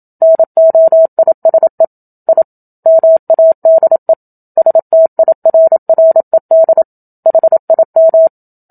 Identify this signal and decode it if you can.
Morse code, heard as a keyed tone, about 27 wpm, 650 Hz. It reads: NOISE I MADE STIRRED HIM
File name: Book Two / 01-Chapter 1 - 27 WPM